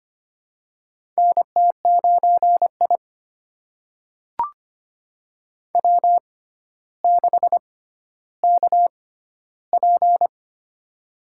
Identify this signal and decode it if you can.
Morse code, heard as a keyed tone, about 25 wpm, spaced as for 13 wpm, 700 Hz. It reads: NT9I W6KP
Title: USA Call Signs - 6x Character Spacing - 25wpm